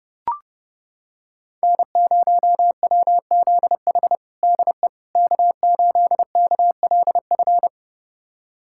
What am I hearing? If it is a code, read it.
Morse code, heard as a keyed tone, about 30 wpm, 700 Hz. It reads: N0WZH DE K8KLF
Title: USA Callsign DE Callsign - 5 Characters - 30wpm